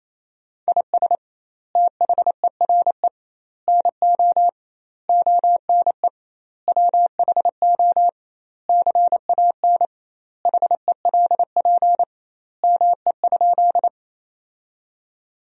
Morse code, heard as a keyed tone, about 28 wpm, 700 Hz. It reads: IS THERE NO ONE WHO CAN HELP ME?
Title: 2x Word-Spacing - Sentences from Top 200 Words - 28wpm